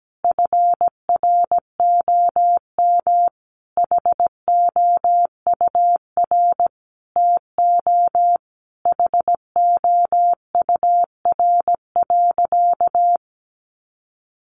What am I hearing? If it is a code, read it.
Morse code, heard as a keyed tone, about 17 wpm, 700 Hz. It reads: FROM HOUR TO HOUR.
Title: Sentences from Top 1200 Words 17wpm